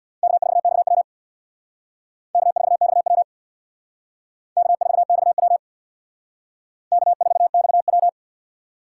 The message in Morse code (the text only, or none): K4XW K4XW K4XW K4XW